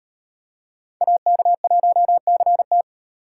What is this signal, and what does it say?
Morse code, heard as a keyed tone, about 36 wpm, 700 Hz. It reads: AK1CT